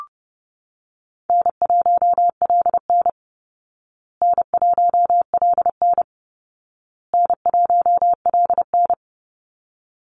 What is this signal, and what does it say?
Morse code, heard as a keyed tone, about 30 wpm, 700 Hz. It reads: N1LN N1LN N1LN
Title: Easy Call Signs Repeated 3x - 30wpm